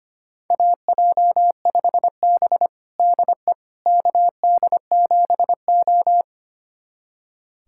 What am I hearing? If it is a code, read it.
Morse code, heard as a keyed tone, about 25 wpm, 700 Hz. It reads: AJ5B DE KD7O
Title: USA Callsign DE Callsign - 4 Characters - 25wpm